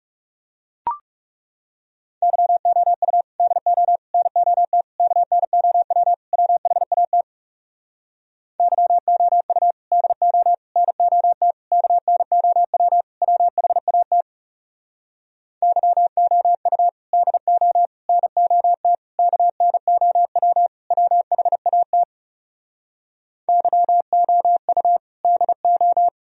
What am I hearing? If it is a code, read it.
Morse code, heard as a keyed tone, about 40 wpm, 700 Hz. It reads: YOU DO NOT KNOW WHAT YOU DO NOT KNOW WHAT YOU DO NOT KNOW WHAT YOU DO